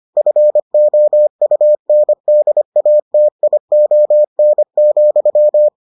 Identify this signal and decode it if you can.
Morse code, heard as a keyed tone, about 25 wpm, 600 Hz. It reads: FOUNDATION,